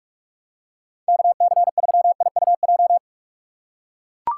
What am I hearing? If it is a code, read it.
Morse code, heard as a keyed tone, about 44 wpm, 700 Hz. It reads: KC3IUJ
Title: USA Call Signs - 6 Characters - 2x Character Spacing - 45wpm